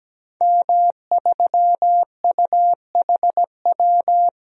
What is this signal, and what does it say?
Morse code, heard as a keyed tone, about 17 wpm, 700 Hz. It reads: M3UHW
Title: DXCC-100 Call Signs - 5 Characters - 17wpm